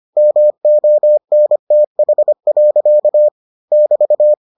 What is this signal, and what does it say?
Morse code, heard as a keyed tone, about 25 wpm, 600 Hz. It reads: MONTH. <BT>